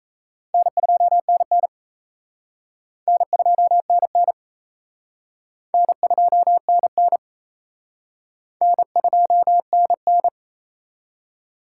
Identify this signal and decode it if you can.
Morse code, heard as a keyed tone, about 34 wpm, 700 Hz. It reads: N2NN N2NN N2NN N2NN
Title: Speed Racing - USA Call Signs 28wpm